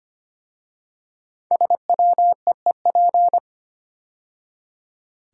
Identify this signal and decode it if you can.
Morse code, heard as a keyed tone, about 25 wpm, 700 Hz. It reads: SWEEP
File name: 5 Letter Words - First 3 Letters Given - 25wpm